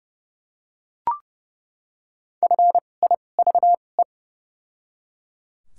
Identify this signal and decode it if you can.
Morse code, heard as a keyed tone, about 31 wpm, spaced as for 22 wpm, 700 Hz. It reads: FIVE
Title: 2x Character Spacing - Top 500 Words - 30wpm